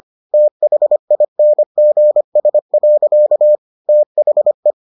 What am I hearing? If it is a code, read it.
Morse code, heard as a keyed tone, about 25 wpm, 600 Hz. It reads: THINGS. THE